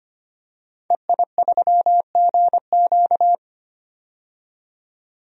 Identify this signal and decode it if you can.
Morse code, heard as a keyed tone, about 25 wpm, 700 Hz. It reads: EI3GQ
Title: DXCC-100 Call Signs - 5 Characters - 25wpm